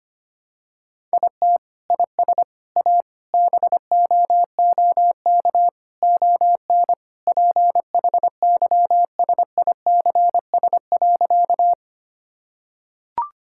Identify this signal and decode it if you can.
Morse code, heard as a keyed tone, about 25 wpm, 700 Hz. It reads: IT IS A BOOK ON PHYSICS.
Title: Sentences from Top 1000 Words - Review Part 5 - 25wpm